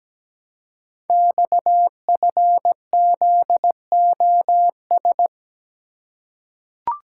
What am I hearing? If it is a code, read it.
Morse code, heard as a keyed tone, about 17 wpm, 700 Hz. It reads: XFZOS
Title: Code Groups - 5 Letters - 17wpm